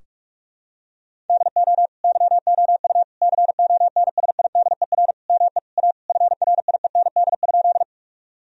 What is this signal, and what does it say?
Morse code, heard as a keyed tone, about 44 wpm, 700 Hz. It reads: DO YOU CONSIDER ME A FRIEND?
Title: Sentences from Top 1000 Words - Review Part 4 - 45wpm